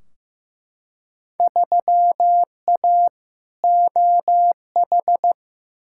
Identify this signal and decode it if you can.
Morse code, heard as a keyed tone, about 15 wpm, 700 Hz. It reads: 3A OH
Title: Field Day Exchanges - 15wpm